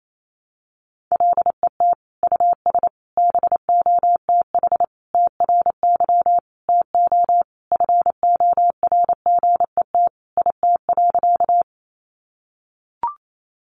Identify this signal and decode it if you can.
Morse code, heard as a keyed tone, about 28 wpm, 700 Hz. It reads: LET US BOTH TRY TO FORGET IT.
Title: Sentences from Top 1200 Words 28wpm